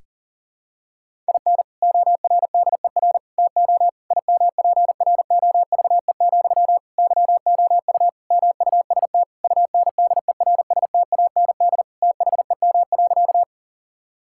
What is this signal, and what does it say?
Morse code, heard as a keyed tone, about 40 wpm, 700 Hz. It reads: IN ORDER TO IMPROVE, YOU MUST UNDERSTAND THEM.
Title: Sentences from Top 1000 Words - Review Part 5 - 40wpm